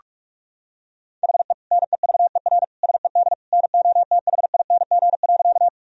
Morse code, heard as a keyed tone, about 41 wpm, 690 Hz. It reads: HE NEVER SED NOTHING.
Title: Sentences from Top 500 Words and Abbreviations 45wpm